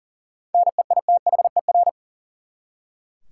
Morse code, heard as a keyed tone, about 40 wpm, 700 Hz. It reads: NEITHER
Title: Seven Letter Words - 40wpm